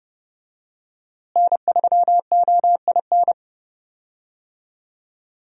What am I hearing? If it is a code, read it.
Morse code, heard as a keyed tone, about 30 wpm, 700 Hz. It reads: N3OIN
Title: USA Call Signs - 5 Characters - Repeated 3x - 30wpm